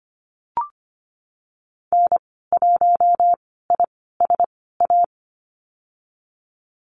Morse code, heard as a keyed tone, about 25 wpm, spaced as for 16 wpm, 700 Hz. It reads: N1ISA
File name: Lesson 15g - Number 5 - Call Signs 25wpm